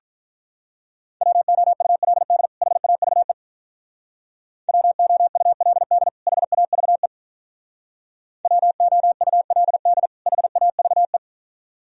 Morse code, heard as a keyed tone, about 46 wpm, 700 Hz. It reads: WOULD HAVE WOULD HAVE WOULD HAVE